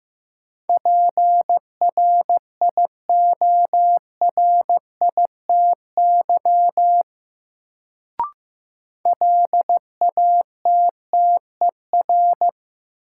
Morse code, heard as a keyed tone, about 15 wpm, 700 Hz. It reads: PRIORITY LATTER
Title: Sentences from Top 1600 Words - New Word Drill - 15wpm